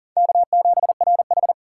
Morse code, heard as a keyed tone, about 40 wpm, 700 Hz. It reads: K7RH
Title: USA Call Signs - 4 Characters - 6x Character Spacing - 40wpm